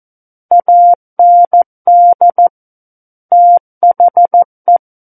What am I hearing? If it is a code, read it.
Morse code, heard as a keyed tone, about 14 wpm, 700 Hz. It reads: AND THE
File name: Book One / 05-Chapter 5 - 14 WPM